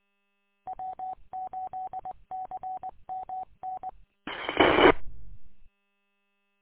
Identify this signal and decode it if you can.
Morse code, heard as a keyed tone, about 20 wpm, 750 Hz. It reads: W8CMN M
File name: w8cmn- -combined